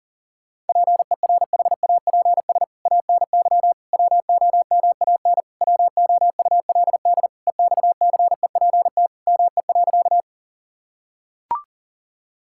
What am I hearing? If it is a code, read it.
Morse code, heard as a keyed tone, about 37 wpm, 700 Hz. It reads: PERHAPS ANY WOMAN WOULD EXCEPT ME.
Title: Sentences from Top 1600 Words 40wpm